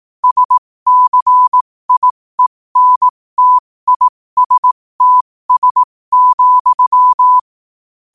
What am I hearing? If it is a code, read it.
Morse code, heard as a keyed tone, about 18 wpm, spaced as for 15 wpm, 1000 Hz. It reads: SCIENTISTS,
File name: qotdfiles / 15wpm